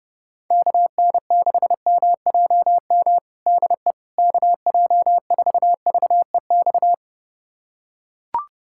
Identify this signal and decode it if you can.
Morse code, heard as a keyed tone, about 30 wpm, 700 Hz. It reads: KN6MJM DE KJ4VEX
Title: USA Callsign DE Callsign - 6 Characters - 30wpm